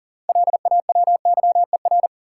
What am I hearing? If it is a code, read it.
Morse code, heard as a keyed tone, about 40 wpm, 700 Hz. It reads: LAWYER
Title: Six Letter Words - 40wpm